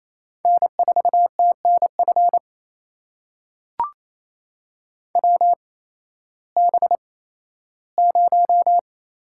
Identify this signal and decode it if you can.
Morse code, heard as a keyed tone, about 28 wpm, spaced as for 15 wpm, 700 Hz. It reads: N4TNF WB0
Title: USA Call Signs - 8x Character Spacing - 28wpm